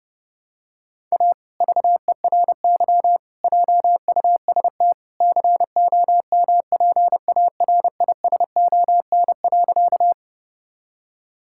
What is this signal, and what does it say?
Morse code, heard as a keyed tone, about 32 wpm, 700 Hz. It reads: A VERY JUST COMPARISON.
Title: Sentences from Top 1600 Words 30wpm